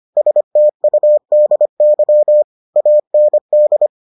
Morse code, heard as a keyed tone, about 25 wpm, 600 Hz. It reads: STUDY AND